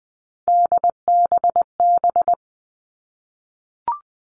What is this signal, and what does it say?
Morse code, heard as a keyed tone, about 20 wpm, 700 Hz. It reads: DBB
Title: All Cut Numbers - 3-Digit 20wpm